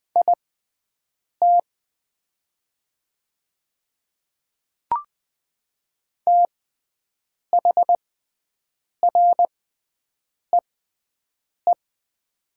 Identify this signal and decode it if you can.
Morse code, heard as a keyed tone, about 20 wpm, spaced as for 7 wpm, 690 Hz. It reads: IT THREE
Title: 6x Character Spacing - Top 300 Words - 20wpm